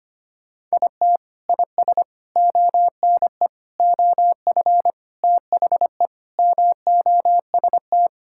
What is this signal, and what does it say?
Morse code, heard as a keyed tone, about 25 wpm, 700 Hz. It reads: IT IS ONE OF THE MOST
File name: practice / 25wpm / Sentences with 3-Word Phrases - 25wpm